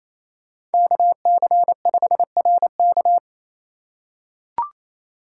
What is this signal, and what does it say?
Morse code, heard as a keyed tone, about 28 wpm, 700 Hz. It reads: KC5RK
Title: USA Call Signs - 5 Characters - 4x Character Spacing - 28wpm